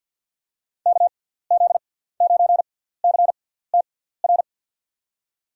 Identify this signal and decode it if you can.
Morse code, heard as a keyed tone, about 52 wpm, spaced as for 17 wpm, 700 Hz. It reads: KZ9CTR
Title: USA Call Signs - 6 Characters - 6x Character Spacing - 50wpm